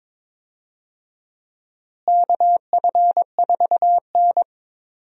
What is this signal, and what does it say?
Morse code, heard as a keyed tone, about 21 wpm, 700 Hz. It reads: KF4N
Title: USA Call Signs - 4 Characters - Repeated 2x - 22wpm